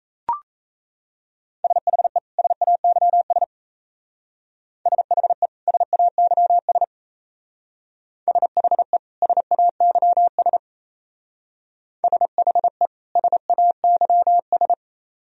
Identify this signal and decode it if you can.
Morse code, heard as a keyed tone, about 34 wpm, 700 Hz. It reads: SHE SAYS SHE SAYS SHE SAYS SHE SAYS
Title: Speed-Racing - Sets of 2 Words 28wpm